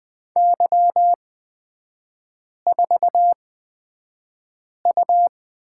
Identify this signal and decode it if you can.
Morse code, heard as a keyed tone, about 20 wpm, spaced as for 5 wpm, 690 Hz. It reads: Y4U
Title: 229 Lesson 37e - Letter x - Three Characters 20wpm